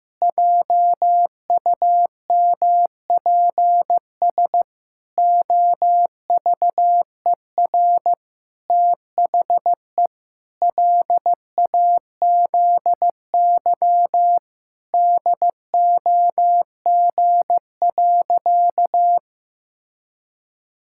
Morse code, heard as a keyed tone, about 15 wpm, 700 Hz. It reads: JUMPS OVER THE LAZY DOG.
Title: Pangram Challenge v2 - 15wpm